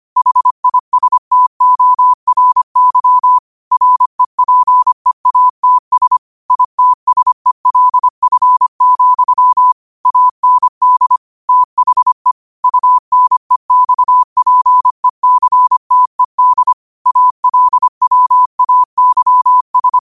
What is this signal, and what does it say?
Morse code, heard as a keyed tone, about 25 wpm, 1000 Hz. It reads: HISTORY REPEATS ITSELF, AND THE UNEXPECTED ALWAYS
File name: qotdfiles / 25wpm